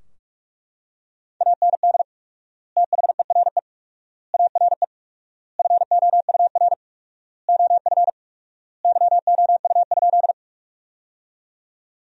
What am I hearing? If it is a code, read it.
Morse code, heard as a keyed tone, about 42 wpm, 700 Hz. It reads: AND THERE ARE FOUR OF YOU?
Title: 4x Word-Spacing - Sentences from Top 300 Words - 45wpm